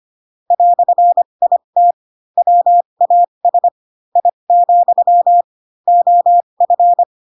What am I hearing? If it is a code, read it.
Morse code, heard as a keyed tone, about 25 wpm, spaced as for 22 wpm, 700 Hz. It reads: "IT WAS I, OF
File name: Pinocchio / Pinocchio ch2 18 25 ik1hge